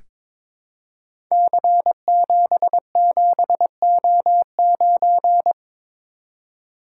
Morse code, heard as a keyed tone, about 22 wpm, 700 Hz. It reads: C77O9
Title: Code Groups - 5 Letter:Numbers - 22wpm